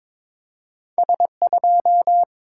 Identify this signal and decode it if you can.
Morse code, heard as a keyed tone, about 22 wpm, 700 Hz. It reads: S2
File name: practice / 22wpm / Two Letter-Number Combinations - 22wpm